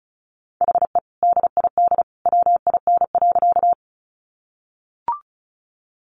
Morse code, heard as a keyed tone, about 34 wpm, 700 Hz. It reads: HE DID WIN.
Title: Sentences from Top 1500 Words 35wpm